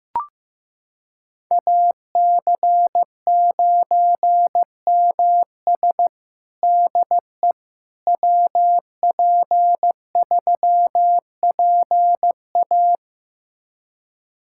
Morse code, heard as a keyed tone, about 15 wpm, 700 Hz. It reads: AC9MS DE WP3PA
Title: USA Callsign DE Callsign - 5 Characters - 15wpm